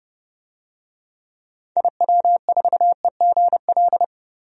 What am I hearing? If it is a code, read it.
Morse code, heard as a keyed tone, about 32 wpm, 690 Hz. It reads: IW4EGL
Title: DXCC-100 Call Signs - 6 Characters - 30wpm